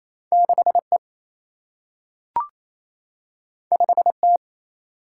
Morse code, heard as a keyed tone, about 28 wpm, 700 Hz. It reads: 6E 5T
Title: Single-Letter Single-Number 28wpm